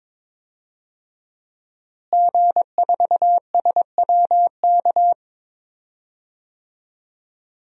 Morse code, heard as a keyed tone, about 22 wpm, 700 Hz. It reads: G4SWK
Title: DXCC-100 Call Signs - 5 Characters - Repeated 3x - 22wpm